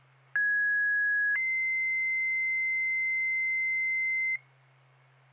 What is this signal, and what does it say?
Two-tone page, heard as a one-second tone, then a three-second tone, about 1650 then 2050 Hz, with noise. It is heard from 0.4 to 4.4 s.